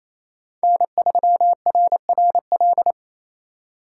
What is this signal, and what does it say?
Morse code, heard as a keyed tone, about 36 wpm, 700 Hz. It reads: N3RRL